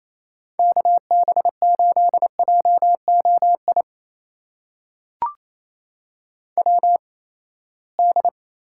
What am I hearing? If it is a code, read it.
Morse code, heard as a keyed tone, about 27 wpm, spaced as for 19 wpm, 700 Hz. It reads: KB8JOI WD
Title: USA Call Signs - 6 Characters - 8x Character Spacing - 28wpm